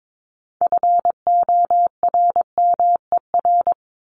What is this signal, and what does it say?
Morse code, heard as a keyed tone, about 22 wpm, 700 Hz. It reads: FORMER
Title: LY Suffix - Top 100 Words with Base Words 22wpm